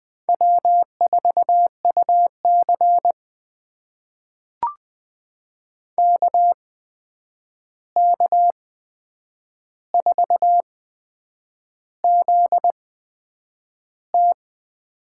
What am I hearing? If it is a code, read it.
Morse code, heard as a keyed tone, about 20 wpm, spaced as for 8 wpm, 700 Hz. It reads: W4UC KK4ZT
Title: USA Call Signs - 8x Character Spacing - 20wpm